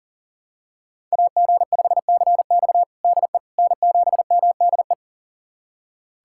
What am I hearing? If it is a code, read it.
Morse code, heard as a keyed tone, about 40 wpm, 700 Hz. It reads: AG5CX DE N7MDE
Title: USA Callsign DE Callsign - 5 Characters - 40wpm